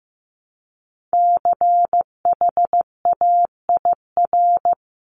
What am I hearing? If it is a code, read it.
Morse code, heard as a keyed tone, about 15 wpm, 700 Hz. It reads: CHAIR
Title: Top 1751-2000 Words 15wpm